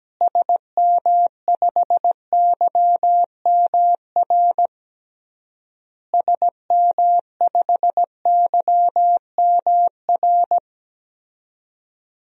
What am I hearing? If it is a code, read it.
Morse code, heard as a keyed tone, about 17 wpm, 700 Hz. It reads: SM5YMR SM5YMR
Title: DXCC-100 Call Signs - 6 Characters - Repeated 3x - 17wpm